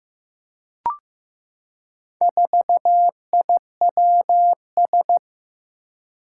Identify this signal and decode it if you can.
Morse code, heard as a keyed tone, about 15 wpm, 700 Hz. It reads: 4IWS
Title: Four Challenging Letter-Number Combinations 15wpm